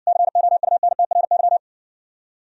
Morse code, heard as a keyed tone, about 60 wpm, 690 Hz. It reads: COUNTRY